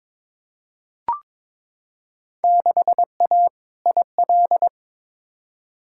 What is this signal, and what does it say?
Morse code, heard as a keyed tone, about 22 wpm, 700 Hz. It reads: 6A IL
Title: Field Day Exchanges - 22wpm